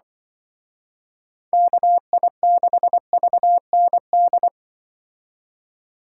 Morse code, heard as a keyed tone, about 24 wpm, 700 Hz. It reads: KI6VND
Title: Speed Racing - USA 2x3 Call Signs 20wpm